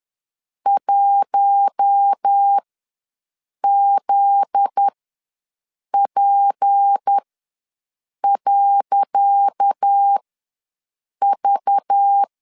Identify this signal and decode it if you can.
Morse code, heard as a keyed tone, about 10 wpm, spaced as for 6 wpm, 800 Hz. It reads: 1ZP.V
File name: LEZIONI / 10 SIGNS / 10C DOT ws 10wpm 4213